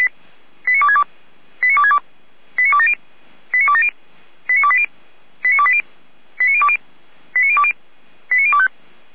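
Five-tone selective call (ZVEI), heard as quick runs of five tones, tones of about 70 ms. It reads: 89262 89262 89270 89270 89280 89280 89020 89020 89025
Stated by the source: ZVEI five-tone call received on SDR.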